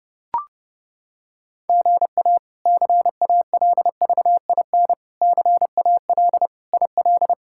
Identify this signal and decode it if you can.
Morse code, heard as a keyed tone, about 32 wpm, 700 Hz. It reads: GA CALVIN CAL IL
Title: K1USN SST QSO Practice - 30wpm